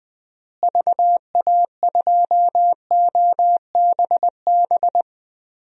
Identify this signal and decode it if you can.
Morse code, heard as a keyed tone, about 20 wpm, 700 Hz. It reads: VA2OBB
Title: DXCC-100 Call Signs - 6 Characters - Repeated 3x - 20wpm